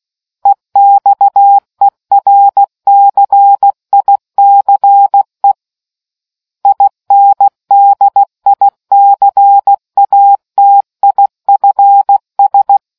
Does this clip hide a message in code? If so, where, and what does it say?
Yes: Morse code from 0.5 to 12.8 s, heard as a keyed tone, about 16 wpm, 800 Hz. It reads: EXERCICE INDICATIFS